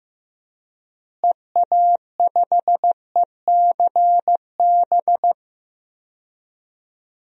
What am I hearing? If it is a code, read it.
Morse code, heard as a keyed tone, about 15 wpm, 700 Hz. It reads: EA5ECB
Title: DXCC-100 Call Signs - 6 Characters - Repeated 3x - 15wpm